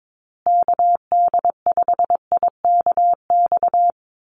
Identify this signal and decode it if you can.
Morse code, heard as a keyed tone, about 22 wpm, 700 Hz. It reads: KD5IKX
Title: USA Call Signs Repeated 2x 22wpm